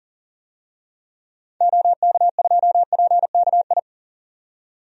Morse code, heard as a keyed tone, about 40 wpm, 700 Hz. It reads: OK2PKI
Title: DXCC-100 Call Signs - 6 Characters - 40wpm